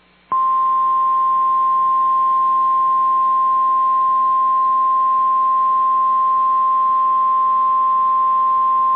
Emergency Alert System (SAME) tones (the 1050 Hz attention tone). It is heard from 1.0 to 7.8 s.
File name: EAS Logs / KMNR